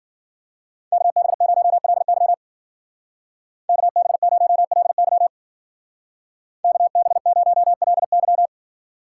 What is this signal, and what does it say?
Morse code, heard as a keyed tone, about 49 wpm, 700 Hz. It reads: KB0LY KB0LY KB0LY
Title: Speed Racing - USA 2x2 Call Signs 40wpm